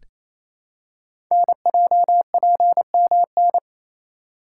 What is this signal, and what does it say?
Morse code, heard as a keyed tone, about 27 wpm, 700 Hz. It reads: NJPMN
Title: Code Groups - 5 Letters - 28wpm